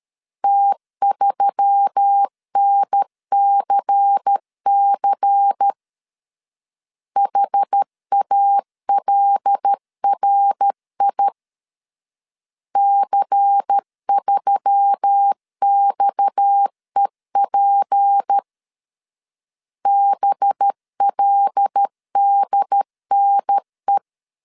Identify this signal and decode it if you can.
Morse code, heard as a keyed tone, about 13 wpm, 800 Hz. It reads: T3NCC HALRI C3XEP BLDNE
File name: CW / DET / 1 6 1500C S A 12wpm 3440